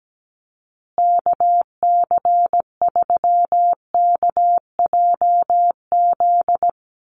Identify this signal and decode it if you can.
Morse code, heard as a keyed tone, about 17 wpm, 700 Hz. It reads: KC3KJZ
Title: USA Call Signs Repeated 2x 17wpm